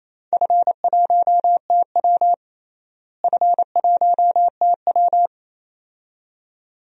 Morse code, heard as a keyed tone, about 28 wpm, 700 Hz. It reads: F1TW F1TW
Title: DXCC-100 Call Signs - 4 Characters - Repeated 2x - 28wpm